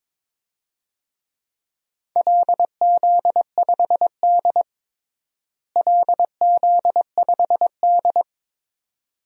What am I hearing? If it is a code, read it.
Morse code, heard as a keyed tone, about 22 wpm, 700 Hz. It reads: LZ5D LZ5D
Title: DXCC-100 Call Signs - 4 Characters - Repeated 2x - 22wpm